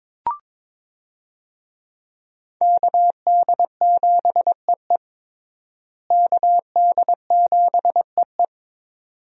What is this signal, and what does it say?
Morse code, heard as a keyed tone, about 22 wpm, 700 Hz. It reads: KD7EE KD7EE
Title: USA Call Signs - 5 Characters - Repeated 2x - 22wpm